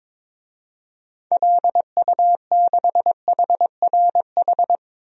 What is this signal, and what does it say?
Morse code, heard as a keyed tone, about 22 wpm, 700 Hz. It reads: LU6HRH
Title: DXCC-100 Call Signs - 6 Characters - 22wpm